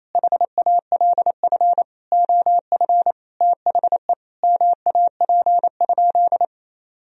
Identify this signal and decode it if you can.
Morse code, heard as a keyed tone, about 28 wpm, 700 Hz. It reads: HALF OF THE MAP?
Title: Sentences from Top 1000 Words - Review Part 2 - 28wpm